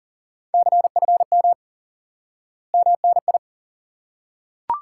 Morse code, heard as a keyed tone, about 40 wpm, 700 Hz. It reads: CFM MNI
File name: IQE Course 3 - Lesson 89 - YAGI - Two Elements - 40wpm